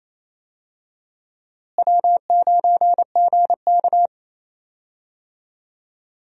Morse code, heard as a keyed tone, about 28 wpm, 700 Hz. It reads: W9GK
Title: USA Call Signs - 4 Characters - Repeated 3x - 28wpm